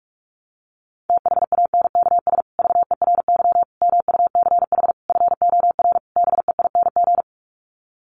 Morse code, heard as a keyed tone, about 44 wpm, 700 Hz. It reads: THANKS VERY MUCH FOR BEING
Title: Sets of 5 Words Encore 45wpm